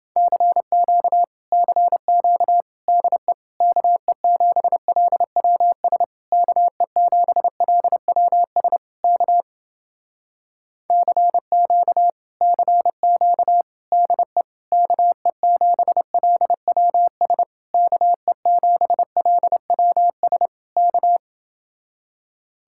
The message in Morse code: CQ CQ DE KE7LWS KE7LWS K CQ CQ DE KE7LWS KE7LWS K

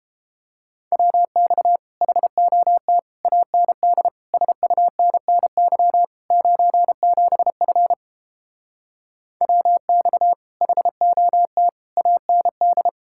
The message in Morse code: WX HOT AND SUNNY 97F WX HOT AND

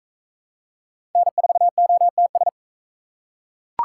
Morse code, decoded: N4OTS

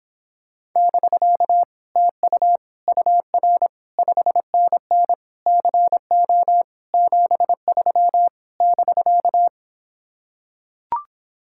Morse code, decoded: <BK> TU UR 5NN CO 73 <BK>

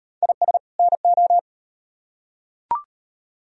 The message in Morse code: IS NO

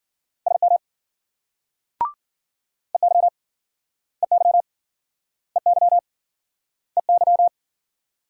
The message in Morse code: SM EY EY EY EY

